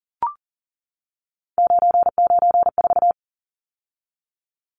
Morse code, heard as a keyed tone, about 40 wpm, 700 Hz. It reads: 994
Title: Numbers - 3-Digit 40wpm